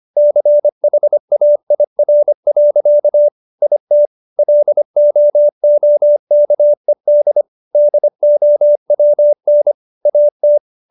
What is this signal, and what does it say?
Morse code, heard as a keyed tone, about 25 wpm, 590 Hz. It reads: CHAIR. IT LOOKED DOWN AT